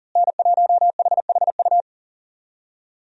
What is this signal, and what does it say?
Morse code, heard as a keyed tone, about 40 wpm, 700 Hz. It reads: N1HHU
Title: USA Call Signs - 5 Characters - Repeated 2x - 40wpm